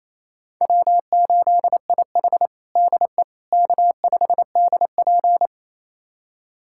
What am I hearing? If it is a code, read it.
Morse code, heard as a keyed tone, about 28 wpm, 700 Hz. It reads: W8IH DE K5DP
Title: USA Callsign DE Callsign - 4 Characters - 28wpm